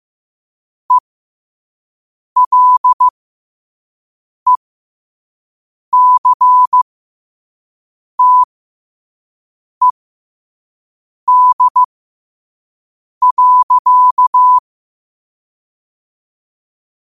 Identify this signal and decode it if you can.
Morse code, heard as a keyed tone, about 15 wpm, spaced as for 5 wpm, 1000 Hz. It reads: ELECTED.
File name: qotdfiles / 5wpm